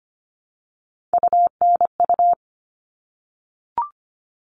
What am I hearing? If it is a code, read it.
Morse code, heard as a keyed tone, about 25 wpm, 700 Hz. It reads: UNU
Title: All Cut Numbers - 3-Digit 25wpm